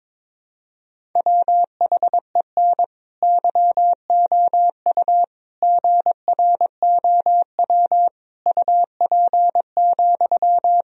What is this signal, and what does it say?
Morse code, heard as a keyed tone, about 22 wpm, 700 Hz. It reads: WHEN YOU GROW UP,
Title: Sentences from Top 1000 Words - Review Part 1 - 22wpm